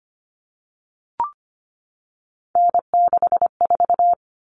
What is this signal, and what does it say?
Morse code, heard as a keyed tone, about 25 wpm, 700 Hz. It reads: N64
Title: Common Cut Numbers - 3-Digit 25wpm